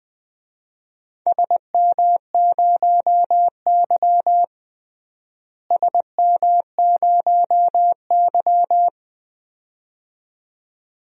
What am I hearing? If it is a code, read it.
Morse code, heard as a keyed tone, about 20 wpm, 700 Hz. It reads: SM0Y SM0Y
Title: DXCC-100 Call Signs - 4 Characters - Repeated 2x - 20wpm